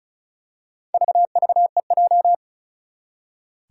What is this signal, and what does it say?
Morse code, heard as a keyed tone, about 34 wpm, 700 Hz. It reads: VVEJ
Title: Four Challenging Letter-Number Combinations 35wpm